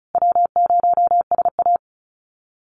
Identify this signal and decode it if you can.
Morse code, heard as a keyed tone, about 34 wpm, 700 Hz. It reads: W0SA
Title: Call Signs Repeated 3x - 35wpm